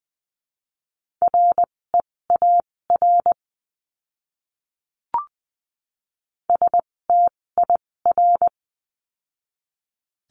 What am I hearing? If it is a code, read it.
Morse code, heard as a keyed tone, about 20 wpm, spaced as for 16 wpm, 700 Hz. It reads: REAR STIR
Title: 256 Lesson 42a - Words 12-20wpm